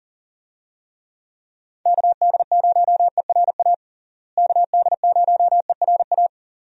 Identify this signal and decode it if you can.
Morse code, heard as a keyed tone, about 37 wpm, 700 Hz. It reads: KD0ERA KD0ERA
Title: USA Call Signs - 6 Characters - Repeated 2x - 40wpm